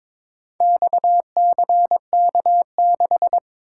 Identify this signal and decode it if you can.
Morse code, heard as a keyed tone, about 22 wpm, 700 Hz. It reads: XCK6
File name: practice / 22wpm / Four Letter-Number Combinations - 22wpm